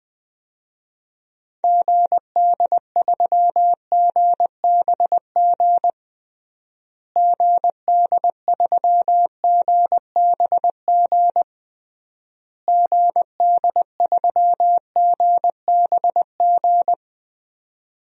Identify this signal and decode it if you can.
Morse code, heard as a keyed tone, about 20 wpm, 700 Hz. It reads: GD3GBG GD3GBG GD3GBG
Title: DXCC-100 Call Signs - 6 Characters - Repeated 3x - 20wpm